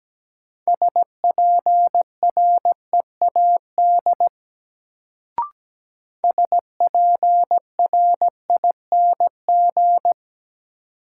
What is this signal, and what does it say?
Morse code, heard as a keyed tone, about 17 wpm, 700 Hz. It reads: SPREAD SPRING
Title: Sentences from Top 1200 Words - New Word Drill Alt - 17wpm